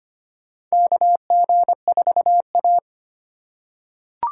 Morse code, heard as a keyed tone, about 25 wpm, 700 Hz. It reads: KG4A
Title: USA Call Signs - 4 Characters - 3x Character Spacing - 25wpm